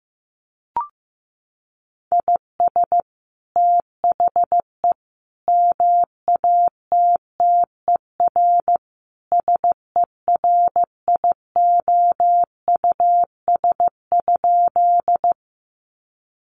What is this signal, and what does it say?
Morse code, heard as a keyed tone, about 15 wpm, 700 Hz. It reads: IS THE MATTER SERIOUS?
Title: Sentences from Top 1100 Words 15wpm